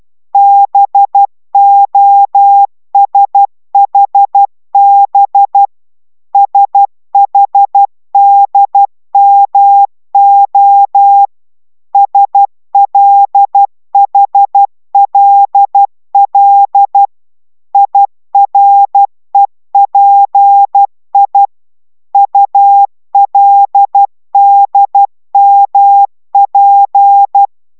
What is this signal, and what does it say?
Morse code, heard as a keyed tone, about 12 wpm, 800 Hz. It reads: BOSHB SHDMO SLHLL IREPI ULDMP